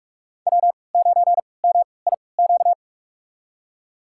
Morse code, heard as a keyed tone, about 46 wpm, spaced as for 25 wpm, 700 Hz. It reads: W9MIQ
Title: USA Call Signs - 5 Characters - 3x Character Spacing - 45wpm